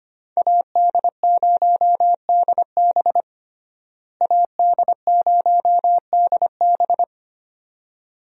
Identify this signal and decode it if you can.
Morse code, heard as a keyed tone, about 25 wpm, 700 Hz. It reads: AD0DB AD0DB
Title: USA Call Signs - 5 Characters - Repeated 2x - 25wpm